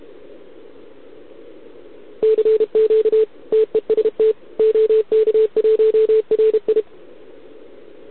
Morse code, heard as a keyed tone, about 31 wpm, 440 Hz. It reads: CQ TEST OK1RI
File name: ARCHIV / OKDX2020 / OK1RIb